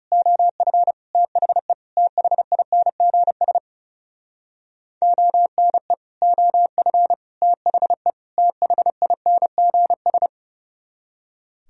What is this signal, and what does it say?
Morse code, heard as a keyed tone, about 40 wpm, 700 Hz. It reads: OF THE THINGS ONE OF THE THINGS